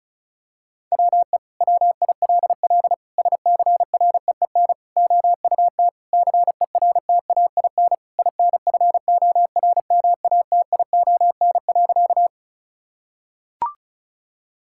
Morse code, heard as a keyed tone, about 36 wpm, 700 Hz. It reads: WE WILL SCREEN OUT CERTAIN INFORMATION.
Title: Sentences from Top 1000 Words - Review Part 6 - 35wpm